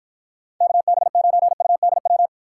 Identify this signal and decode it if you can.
Morse code, heard as a keyed tone, about 52 wpm, 700 Hz. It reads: KB9UDW